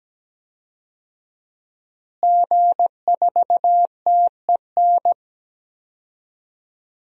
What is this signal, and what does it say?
Morse code, heard as a keyed tone, about 17 wpm, 700 Hz. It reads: G4TEN
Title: DXCC-100 Call Signs - 5 Characters - Repeated 2x - 17wpm